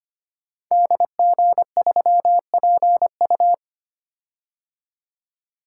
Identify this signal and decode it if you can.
Morse code, heard as a keyed tone, about 25 wpm, 700 Hz. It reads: DG3PU
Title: Code Groups - 5 Letter:Numbers - 25wpm